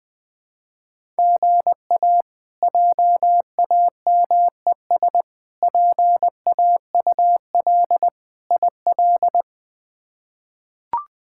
Morse code, heard as a keyed tone, about 20 wpm, 700 Hz. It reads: GA JAMES PAUL IL
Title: K1USN SST QSO Practice - 20wpm